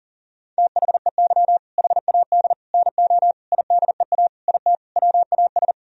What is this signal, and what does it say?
Morse code, heard as a keyed tone, about 40 wpm, 700 Hz. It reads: THEY HAD NO IDEA IT WAS
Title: Sentences from Top 1000 Words - Review Part 1 - 40wpm